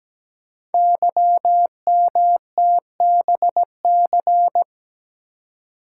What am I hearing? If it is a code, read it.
Morse code, heard as a keyed tone, about 17 wpm, 700 Hz. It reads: YMTBC
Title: Code Groups - 5 Letters - 17wpm